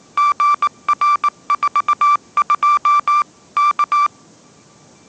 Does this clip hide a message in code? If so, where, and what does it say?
Yes: Morse code from 0.2 to 4.1 s, heard as a keyed tone, about 18 wpm, 1170 Hz. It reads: GR42 K